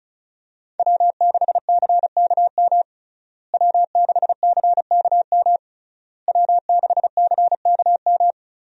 Morse code, heard as a keyed tone, about 36 wpm, 700 Hz. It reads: W6CKM W6CKM W6CKM
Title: USA Call Signs - 5 Characters - Repeated 3x - 35wpm